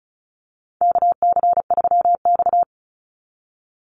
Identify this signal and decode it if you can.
Morse code, heard as a keyed tone, about 34 wpm, 700 Hz. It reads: KC3X
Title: Call Signs Repeated 3x - 35wpm